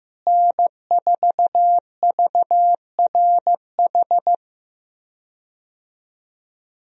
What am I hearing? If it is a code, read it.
Morse code, heard as a keyed tone, about 15 wpm, 700 Hz. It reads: N4VRH
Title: USA Call Signs - 5 Characters - Repeated 2x - 15wpm